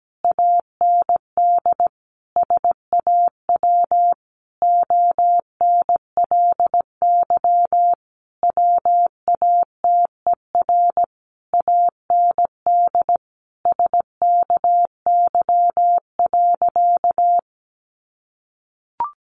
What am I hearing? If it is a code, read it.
Morse code, heard as a keyed tone, about 17 wpm, 700 Hz. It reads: AND SAW ONLY WATER AND SKY.
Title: Sentences from Top 1200 Words 17wpm